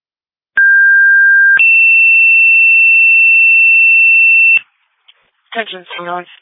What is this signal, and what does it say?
Two-tone page, heard as a one-second tone, then a three-second tone, about 1600 then 2700 Hz, with speech. It is heard from 0.6 to 4.6 s.